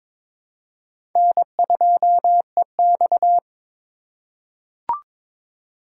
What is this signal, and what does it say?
Morse code, heard as a keyed tone, about 21 wpm, 700 Hz. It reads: N2EX
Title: USA Call Signs - 4x Character Spacing - 22wpm